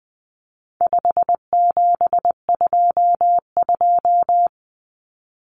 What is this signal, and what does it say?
Morse code, heard as a keyed tone, about 20 wpm, 700 Hz. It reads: 5722
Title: Common Cut Numbers - 4-Digit 20wpm